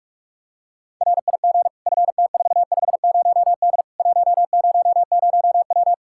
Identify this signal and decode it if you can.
Morse code, heard as a keyed tone, about 45 wpm, 700 Hz. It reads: RIG FT450D 100W
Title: Practice QSO - Exchange 2 - Info - 45wpm